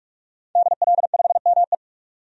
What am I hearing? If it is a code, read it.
Morse code, heard as a keyed tone, about 46 wpm, 700 Hz. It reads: DL5ME
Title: DXCC-100 Call Signs - 5 Characters - 45wpm